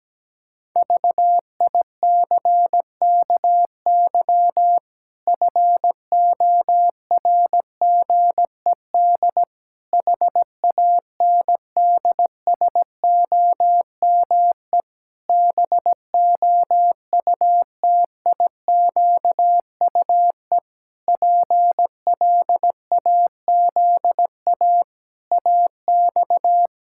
Morse code, heard as a keyed tone, about 17 wpm, 700 Hz. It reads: VICKY FORGED HANDSOME BOUTIQUE PLAZA AX
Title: Pangram Challenge v2 - 17wpm